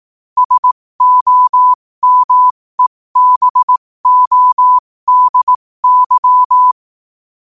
Morse code, heard as a keyed tone, about 18 wpm, spaced as for 15 wpm, 1000 Hz. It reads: SOMEBODY
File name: qotdfiles / 15wpm